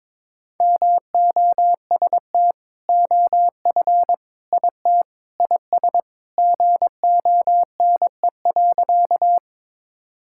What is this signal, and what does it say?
Morse code, heard as a keyed tone, about 22 wpm, 700 Hz. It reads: MOST OF IT IS GONE.
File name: practice / 22wpm / Sentences with 2-Word Phrases - 22wpm